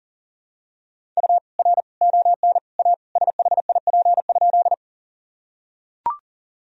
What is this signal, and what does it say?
Morse code, heard as a keyed tone, about 40 wpm, 700 Hz. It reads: U R ON A SHIP?
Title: Sentences from Top 400 Words and Abbreviations 40wpm